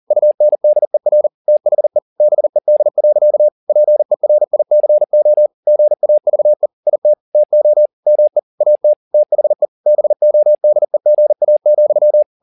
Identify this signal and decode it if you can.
Morse code, heard as a keyed tone, about 40 wpm, 600 Hz. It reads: UNDER THE BED. PERICO GAVE IT TO ME AT THE BODEGA,